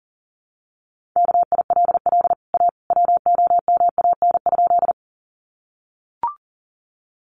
Morse code, heard as a keyed tone, about 40 wpm, 700 Hz. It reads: KILL A WOMAN?
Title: Sentences from Top 1400 Words 40wpm